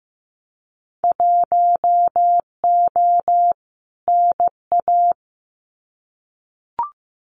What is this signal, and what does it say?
Morse code, heard as a keyed tone, about 15 wpm, 700 Hz. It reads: 1O NA
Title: Winter Field Day Exchanges 15wpm